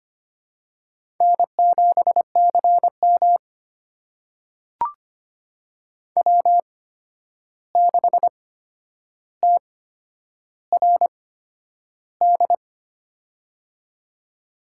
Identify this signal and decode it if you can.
Morse code, heard as a keyed tone, about 25 wpm, spaced as for 10 wpm, 700 Hz. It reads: N7CM W6TRD
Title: USA Call Signs - 8x Character Spacing - 25wpm